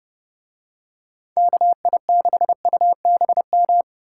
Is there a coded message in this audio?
Yes: Morse code from 1.4 to 3.8 s, heard as a keyed tone, about 29 wpm, 700 Hz. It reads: KI6UBM